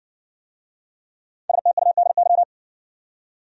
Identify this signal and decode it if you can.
Morse code, heard as a keyed tone, about 69 wpm, 700 Hz. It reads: STUDY